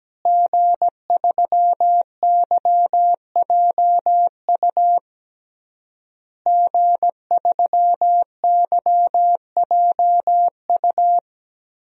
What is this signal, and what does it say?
Morse code, heard as a keyed tone, about 17 wpm, 700 Hz. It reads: G3YJU G3YJU
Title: DXCC-100 Call Signs - 5 Characters - Repeated 3x - 17wpm